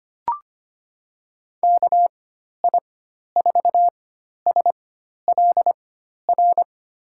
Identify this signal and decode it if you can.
Morse code, heard as a keyed tone, about 25 wpm, spaced as for 12 wpm, 700 Hz. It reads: KI4SLR
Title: USA Call Signs - 6 Characters - 4x Character Spacing - 25wpm